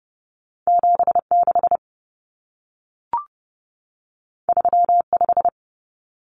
Morse code, heard as a keyed tone, about 30 wpm, 700 Hz. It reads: 76 35
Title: Common Cut Numbers - 2-Digit 30wpm